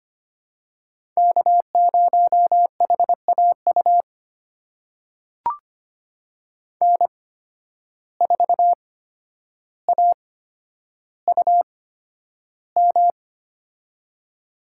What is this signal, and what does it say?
Morse code, heard as a keyed tone, about 25 wpm, spaced as for 11 wpm, 700 Hz. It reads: K0HAU N4AUM
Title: USA Call Signs - 5 Characters - 8x Character Spacing - 25wpm